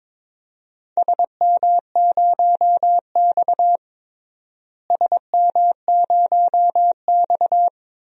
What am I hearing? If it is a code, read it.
Morse code, heard as a keyed tone, about 21 wpm, 700 Hz. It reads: SM0X SM0X
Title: DXCC-100 Call Signs - 4 Characters - Repeated 2x - 22wpm